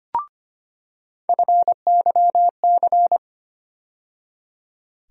Morse code, heard as a keyed tone, about 25 wpm, 700 Hz. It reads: FYC
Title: USA Call Sign Postfixes - 3 Letters 25wpm